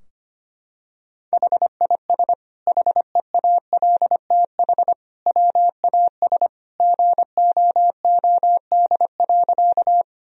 Morse code, heard as a keyed tone, about 25 wpm, 700 Hz. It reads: HIS HEALTH WAS GOOD.
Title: Sentences from Top 1000 Words - Review Part 3 - 25wpm